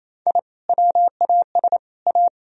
Morse code, heard as a keyed tone, about 35 wpm, 700 Hz. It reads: I WAS A
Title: Speed-Racing - Sets of 3 Words 28wpm